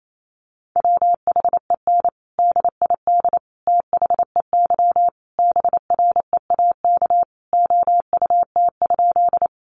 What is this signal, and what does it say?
Morse code, heard as a keyed tone, about 28 wpm, 700 Hz. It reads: WHEN DID THEY BREAK OUT?
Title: Sentences from Top 900 Words 28wpm